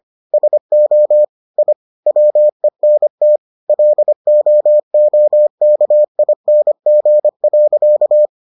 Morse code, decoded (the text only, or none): SO I WENT LOOKING.